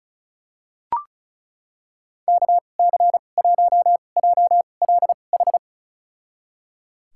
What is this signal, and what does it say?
Morse code, heard as a keyed tone, about 36 wpm, spaced as for 25 wpm, 700 Hz. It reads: KC1JLH
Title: USA Call Signs - 2x Character Spacing - 35wpm